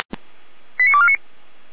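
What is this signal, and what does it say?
Five-tone selective call (ZVEI), heard as quick runs of five tones, tones of about 70 ms. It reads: 89249
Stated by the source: ZVEI five-tone selective call received by SDR.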